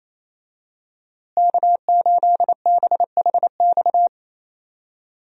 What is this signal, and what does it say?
Morse code, heard as a keyed tone, about 28 wpm, 700 Hz. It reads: K8BHX
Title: USA Call Signs - 5 Characters - Repeated 3x - 28wpm